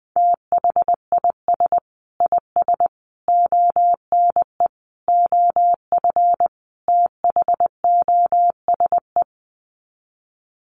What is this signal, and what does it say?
Morse code, heard as a keyed tone, about 20 wpm, 700 Hz. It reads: THIS IS ONE OF THOSE
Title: Sets of 5 Words Encore2 20wpm